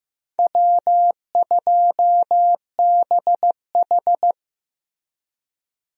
Morse code, heard as a keyed tone, about 15 wpm, 700 Hz. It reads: W2BH
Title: USA Call Signs - 4 Characters - Repeated 3x - 15wpm